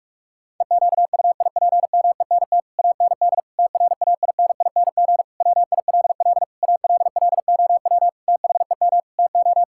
Morse code, heard as a keyed tone, about 46 wpm, 700 Hz. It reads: EQUIPMENT AND TRAINING WILL ALLOW THEM TO
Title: Sentences from Top 1000 Words - Review Part 5 - 45wpm